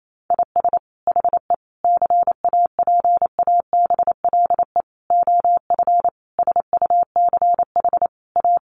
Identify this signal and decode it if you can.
Morse code, heard as a keyed tone, about 28 wpm, 700 Hz. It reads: IS HE CAPABLE OF SUCH A
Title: Sentences from Top 1200 Words 28wpm